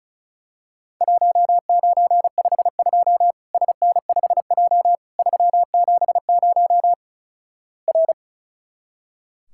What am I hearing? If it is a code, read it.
Morse code, heard as a keyed tone, about 36 wpm, 700 Hz. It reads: 1952 SN5J 370 R
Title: Worked All Europe DX Contest - QTC Exchange - 35wpm